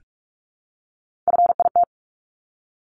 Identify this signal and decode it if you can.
Morse code, heard as a keyed tone, about 44 wpm, 700 Hz. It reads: FIT
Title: Top 751-1000 Words 45wpm